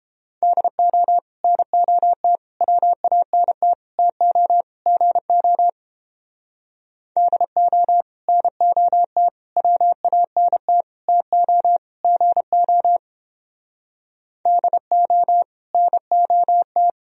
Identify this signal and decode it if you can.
Morse code, heard as a keyed tone, about 31 wpm, 700 Hz. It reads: DO NOT WANT TO GO DO NOT WANT TO GO DO NOT